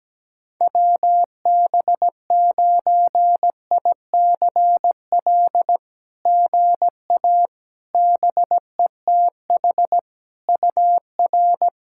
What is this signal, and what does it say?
Morse code, heard as a keyed tone, about 17 wpm, 700 Hz. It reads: WB9ICL GA BETH UR
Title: SOTA QSO Practice 17wpm